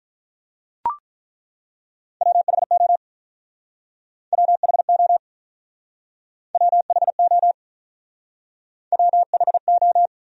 Morse code, heard as a keyed tone, about 48 wpm, 700 Hz. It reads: WHO WHO WHO WHO